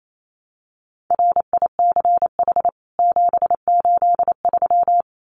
Morse code, heard as a keyed tone, about 28 wpm, 700 Hz. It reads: RICH 783
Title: CWT Exchange 28wpm